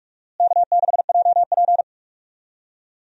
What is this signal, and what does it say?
Morse code, heard as a keyed tone, about 42 wpm, 700 Hz. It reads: K6JP
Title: USA Call Signs - 4 Characters - Repeated 3x - 45wpm